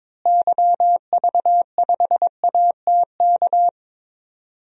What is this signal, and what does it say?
Morse code, heard as a keyed tone, about 22 wpm, 700 Hz. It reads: YV5ATK
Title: DXCC-100 Call Signs - 6 Characters - Repeated 3x - 22wpm